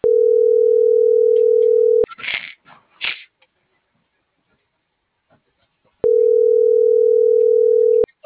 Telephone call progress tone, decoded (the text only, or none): ringback tone